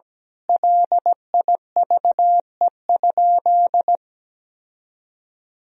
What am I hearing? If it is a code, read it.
Morse code, heard as a keyed tone, about 17 wpm, 700 Hz. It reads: LIVE?
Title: Sentences from Top 1000 Words - Review Part 4 - 17wpm